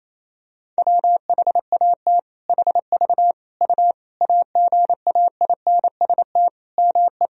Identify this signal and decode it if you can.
Morse code, heard as a keyed tone, about 28 wpm, 700 Hz. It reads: WHAT HV U AGAINST ME
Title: Sentences from Top 400 Words and Abbreviations 28wpm